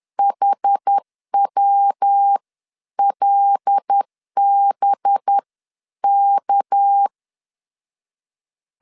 Morse code, heard as a keyed tone, about 10 wpm, spaced as for 9 wpm, 800 Hz. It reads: HWLBK
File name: LEZIONI / 10 SIGNS / 10B INT 10wpm 2844